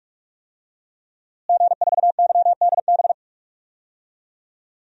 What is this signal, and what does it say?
Morse code, heard as a keyed tone, about 43 wpm, 700 Hz. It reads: G4YDB
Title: DXCC-100 Call Signs - 5 Characters - Repeated 2x - 45wpm